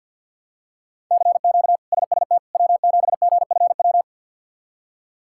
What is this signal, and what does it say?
Morse code, heard as a keyed tone, about 50 wpm, 700 Hz. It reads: CQ SST W7GFW
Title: K1USN SST QSO Practice - 50wpm